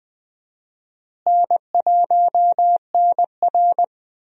Morse code, heard as a keyed tone, about 20 wpm, 700 Hz. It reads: N1NR
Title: USA Call Signs - 4 Characters - Repeated 2x - 20wpm